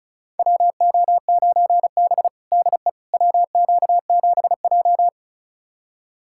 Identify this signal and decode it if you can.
Morse code, heard as a keyed tone, about 34 wpm, 700 Hz. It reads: WO9B DE WQ7J
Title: USA Callsign DE Callsign - 4 Characters - 35wpm